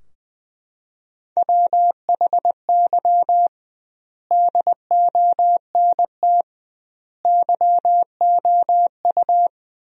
Morse code, heard as a keyed tone, about 20 wpm, 700 Hz. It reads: WHY DONT YOU
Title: 2x Word-Spacing - Sentences from Top 300 Words - 20wpm